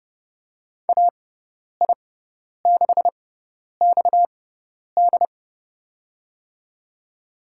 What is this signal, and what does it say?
Morse code, heard as a keyed tone, about 30 wpm, spaced as for 10 wpm, 700 Hz. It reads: AI6XD
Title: USA Call Signs - 5 Characters - 6x Character Spacing - 30wpm